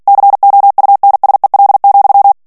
Morse code, heard as a keyed tone, about 48 wpm, 800 Hz. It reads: COUNSEL,
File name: ANGLAIS texte clair 3 vitesse 45 mots minute